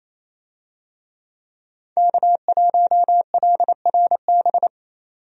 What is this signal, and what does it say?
Morse code, heard as a keyed tone, about 28 wpm, 700 Hz. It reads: K1LRB
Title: USA Call Signs - 5 Characters - 28wpm